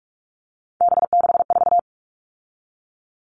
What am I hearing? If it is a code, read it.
Morse code, heard as a keyed tone, about 44 wpm, 700 Hz. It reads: B64
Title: All Cut Numbers - 3-Digit 45wpm